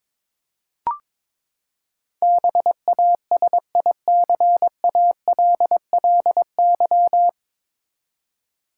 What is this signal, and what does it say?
Morse code, heard as a keyed tone, about 22 wpm, 700 Hz. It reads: BASICALLY
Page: Morse Code Ninja practice 22wpm